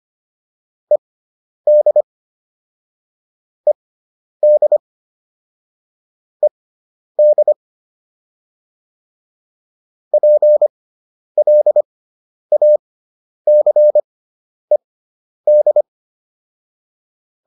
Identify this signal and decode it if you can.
Morse code, heard as a keyed tone, about 25 wpm, spaced as for 10 wpm, 600 Hz. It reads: ED ED ED PLACED